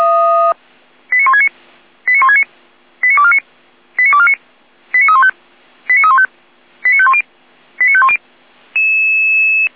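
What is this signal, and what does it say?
Five-tone selective call (ZVEI), heard as quick runs of five tones, tones of about 70 ms. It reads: 89169 89169 89249 89249 89315 89315 89510 89510
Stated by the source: ZVEI five-tone selective call 89337 received by SDR.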